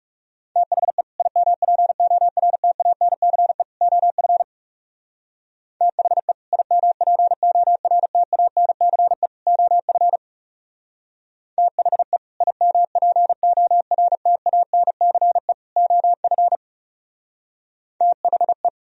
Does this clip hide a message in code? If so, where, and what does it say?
Yes: Morse code from 0.6 to 18.7 s, heard as a keyed tone, about 39 wpm, 700 Hz. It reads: THE IMPORTANCE OF THE IMPORTANCE OF THE IMPORTANCE OF THE